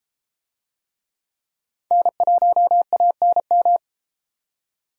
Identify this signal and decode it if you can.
Morse code, heard as a keyed tone, about 30 wpm, 700 Hz. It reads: N1ANM